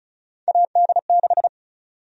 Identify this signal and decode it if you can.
Morse code, heard as a keyed tone, about 36 wpm, 700 Hz. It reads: AD6
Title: USA Call Sign Prefixes - Repeated 2x - 35wpm